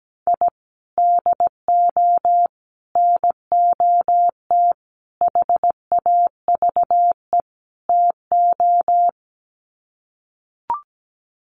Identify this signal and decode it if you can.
Morse code, heard as a keyed tone, about 17 wpm, 700 Hz. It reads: I DO NOT HAVE TO
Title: Sets of 5 Words 17wpm